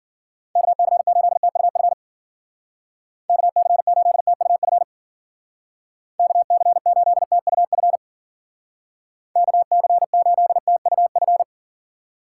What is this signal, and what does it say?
Morse code, heard as a keyed tone, about 54 wpm, 700 Hz. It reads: KC8TUF KC8TUF KC8TUF KC8TUF